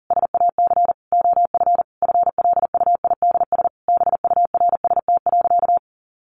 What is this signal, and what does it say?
Morse code, heard as a keyed tone, about 40 wpm, 700 Hz. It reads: SAC OF FLUIDS BURST.
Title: A Sound Of Thunder 40wpm